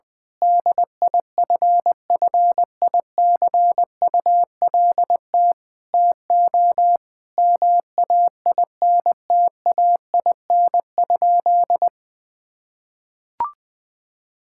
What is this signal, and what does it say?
Morse code, heard as a keyed tone, about 20 wpm, 700 Hz. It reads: DIFFICULT TO MAINTAIN?
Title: Sentences from Top 1000 Words - Review Part 5 - 20wpm